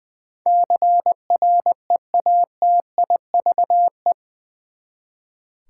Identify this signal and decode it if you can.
Morse code, heard as a keyed tone, about 20 wpm, 700 Hz. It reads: CREATIVE
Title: Eight Letter Words - 20wpm